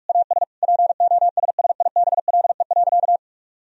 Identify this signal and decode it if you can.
Morse code, heard as a keyed tone, about 43 wpm, 700 Hz. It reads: AS POSSIBLE.
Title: Sentences from Top 1000 Words - Review Part 4 - 45wpm